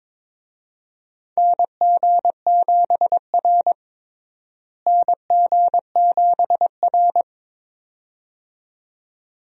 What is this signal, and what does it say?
Morse code, heard as a keyed tone, about 21 wpm, 700 Hz. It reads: NG7R NG7R
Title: USA Call Signs - 4 Characters - Repeated 2x - 22wpm